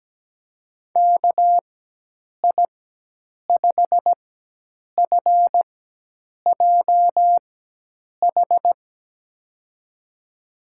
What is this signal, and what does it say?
Morse code, heard as a keyed tone, about 17 wpm, spaced as for 8 wpm, 700 Hz. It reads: KI5FJH
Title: USA Call Signs - 6 Characters - 4x Character Spacing - 17wpm